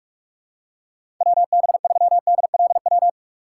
Morse code, heard as a keyed tone, about 42 wpm, 700 Hz. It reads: WB3DLW
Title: USA Call Signs - 6 Characters - 4x Character Spacing - 45wpm